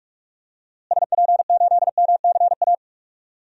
Morse code, heard as a keyed tone, about 44 wpm, 700 Hz. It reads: SP8MCA
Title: DXCC-100 Call Signs - 6 Characters - Repeated 3x - 45wpm